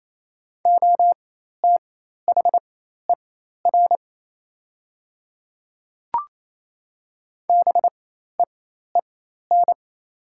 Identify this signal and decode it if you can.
Morse code, heard as a keyed tone, about 28 wpm, spaced as for 13 wpm, 700 Hz. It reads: OTHER BEEN
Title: 4x Character Spacing - Top 100 Words - 28wpm